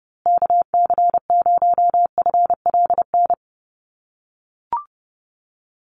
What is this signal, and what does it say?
Morse code, heard as a keyed tone, about 29 wpm, 700 Hz. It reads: KC0FLN
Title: USA Call Signs Repeated 3x 30wpm